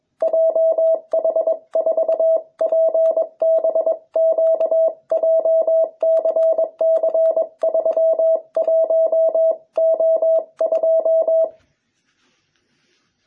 Morse code, heard as a keyed tone, about 22 wpm, 650 Hz. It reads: JH4PBQ JCC31O2